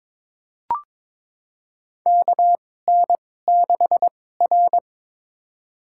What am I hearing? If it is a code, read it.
Morse code, heard as a keyed tone, about 22 wpm, spaced as for 16 wpm, 700 Hz. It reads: KN6R
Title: USA Call Signs - 4 Characters - 2x Character Spacing - 22wpm